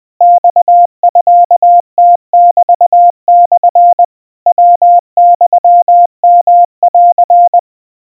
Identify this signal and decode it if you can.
Morse code (Japanese Wabun code), heard as a keyed tone, about 21 wpm, 700 Hz. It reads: マミムメモ ヤユヨン
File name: kawa / Morse sample / aiueo 70 cpm 700 Hz Crispy